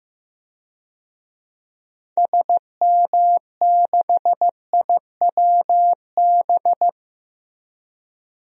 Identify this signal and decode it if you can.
Morse code, heard as a keyed tone, about 15 wpm, 700 Hz. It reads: SM6IWB
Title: DXCC-100 Call Signs - 6 Characters - Repeated 2x - 15wpm